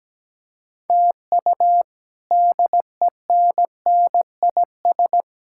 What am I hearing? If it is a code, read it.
Morse code, heard as a keyed tone, about 17 wpm, 700 Hz. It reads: TU DENNIS
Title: K1USN SST QSO Practice - 17wpm